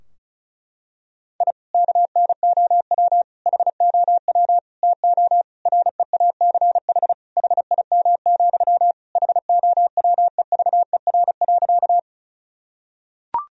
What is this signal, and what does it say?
Morse code, heard as a keyed tone, about 34 wpm, 700 Hz. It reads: I KNOW HOW TO REACH HIM, HOWEVER.
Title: Sentences from Top 1000 Words - Review Part 2 - 35wpm